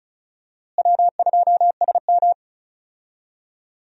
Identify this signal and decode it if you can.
Morse code, heard as a keyed tone, about 36 wpm, 700 Hz. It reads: W2SM
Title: USA Call Signs - 4 Characters - Repeated 3x - 35wpm